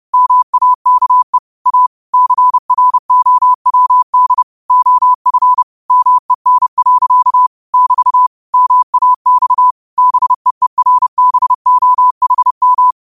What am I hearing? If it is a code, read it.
Morse code, heard as a keyed tone, about 30 wpm, 1000 Hz. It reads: MAKE A CROWD OF MEN. <BT> MAX BEERBOHM
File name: qotdfiles / 30wpm